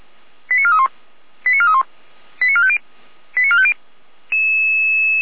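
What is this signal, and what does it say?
Five-tone selective call (ZVEI), heard as quick runs of five tones, tones of about 70 ms. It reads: 89431 89431 89450 89450
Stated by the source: ZVEI five-tone call 88251 received by SDR.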